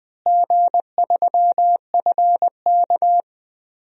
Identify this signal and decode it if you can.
Morse code, heard as a keyed tone, about 20 wpm, 700 Hz. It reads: G3FK
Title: DXCC-100 Call Signs - 4 Characters - Repeated 2x - 20wpm